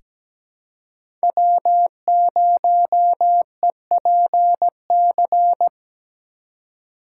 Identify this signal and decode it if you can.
Morse code, heard as a keyed tone, about 17 wpm, 700 Hz. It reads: W0EPC
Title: K1USN SST QSO Practice - 17wpm